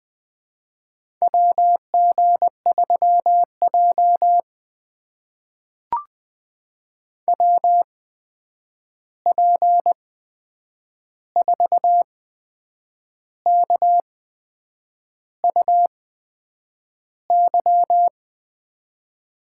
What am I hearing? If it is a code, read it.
Morse code, heard as a keyed tone, about 20 wpm, spaced as for 8 wpm, 700 Hz. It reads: WG3J WP4KUY
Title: USA Call Signs - 8x Character Spacing - 20wpm